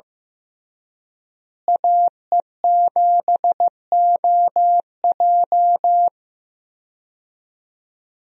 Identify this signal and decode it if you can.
Morse code, heard as a keyed tone, about 16 wpm, 700 Hz. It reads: AE7OJ
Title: USA Call Signs - 5 Characters - Repeated 3x - 15wpm